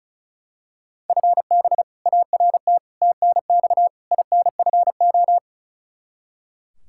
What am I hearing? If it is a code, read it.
Morse code, heard as a keyed tone, about 34 wpm, 700 Hz. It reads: FB ART TNX INFO
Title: K1USN SST QSO Practice 35wpm